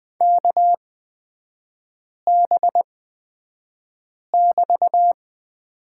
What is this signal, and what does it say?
Morse code, heard as a keyed tone, about 20 wpm, spaced as for 5 wpm, 700 Hz. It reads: KB<BT>
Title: 224 Lesson 36e - BT - Three Characters 20wpm